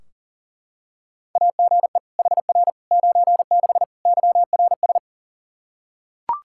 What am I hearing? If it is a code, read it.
Morse code, heard as a keyed tone, about 40 wpm, 700 Hz. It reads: AGE HR 96 YRS
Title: Practice QSO - Exchange 3 - More Info - 40wpm